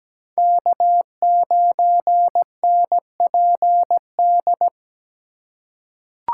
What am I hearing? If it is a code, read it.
Morse code, heard as a keyed tone, about 17 wpm, 700 Hz. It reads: K9NPD
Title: USA Call Signs - 2x Character Spacing - 17wpm